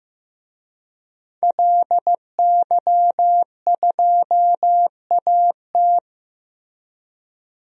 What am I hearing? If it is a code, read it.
Morse code, heard as a keyed tone, about 15 wpm, 700 Hz. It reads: LY2AT
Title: DXCC-100 Call Signs - 5 Characters - Repeated 2x - 15wpm